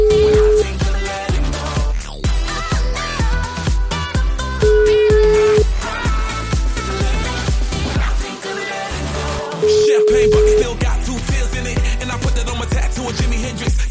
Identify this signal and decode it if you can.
Ringback tone (Europe, 425 Hz), with noise.